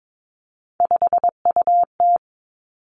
Morse code, heard as a keyed tone, about 22 wpm, 700 Hz. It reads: 5UT
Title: Three Challenging Letter-Number Combinations 22wpm